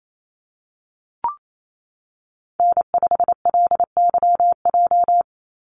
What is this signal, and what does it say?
Morse code, heard as a keyed tone, about 28 wpm, 700 Hz. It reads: N5LYJ
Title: Call Signs Repeated 3x - 28wpm